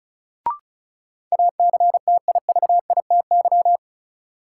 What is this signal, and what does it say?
Morse code, heard as a keyed tone, about 34 wpm, 700 Hz. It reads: ACTIVITY
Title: Sentences from Top 1000 Words - Review Part 4 - New Word Drill - 35wpm